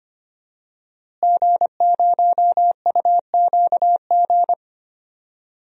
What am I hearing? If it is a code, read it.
Morse code, heard as a keyed tone, about 25 wpm, 700 Hz. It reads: G0UQG
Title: DXCC-100 Call Signs - 5 Characters - Repeated 2x - 25wpm